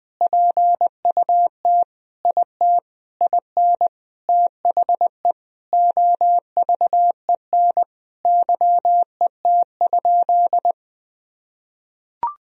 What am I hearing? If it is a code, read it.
Morse code, heard as a keyed tone, about 21 wpm, 700 Hz. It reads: PUT IT IN THE OVEN YET?
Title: Sentences from Top 1000 Words - Review Part 4 - 20wpm